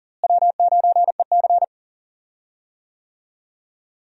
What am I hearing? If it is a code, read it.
Morse code, heard as a keyed tone, about 40 wpm, 700 Hz. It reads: W9EC
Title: USA Call Signs - 4 Characters - Repeated 2x - 40wpm